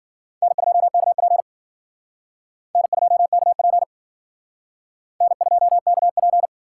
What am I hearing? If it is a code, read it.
Morse code, heard as a keyed tone, about 52 wpm, 700 Hz. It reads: N2KP N2KP N2KP